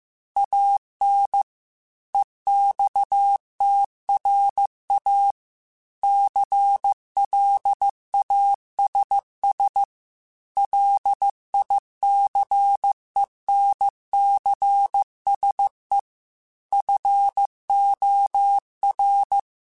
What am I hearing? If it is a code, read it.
Morse code, heard as a keyed tone, about 15 wpm, 800 Hz. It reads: AN EXTRA CLASS LICENCSE FOR
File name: QSOs / QSOs-mp3s / kc7sac@13